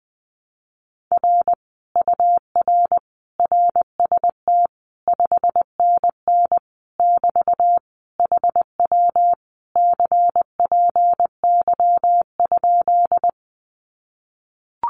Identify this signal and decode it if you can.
Morse code, heard as a keyed tone, about 20 wpm, 700 Hz. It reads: R UR RST 5NN <BT> HW CPY?
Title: Common QSO Phrases Encore 20wpm